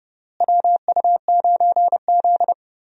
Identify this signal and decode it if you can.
Morse code, heard as a keyed tone, about 30 wpm, 700 Hz. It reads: WU9Z
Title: USA Call Signs - 4 Characters - Repeated 3x - 30wpm